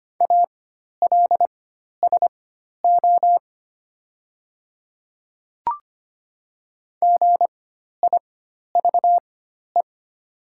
Morse code, heard as a keyed tone, about 25 wpm, spaced as for 12 wpm, 700 Hz. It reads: ALSO GIVE
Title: 4x Character Spacing - Top 300 Words - 25wpm